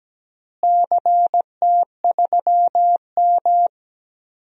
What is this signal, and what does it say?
Morse code, heard as a keyed tone, about 17 wpm, 700 Hz. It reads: CT3M
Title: DXCC-100 Call Signs - 4 Characters - Repeated 2x - 17wpm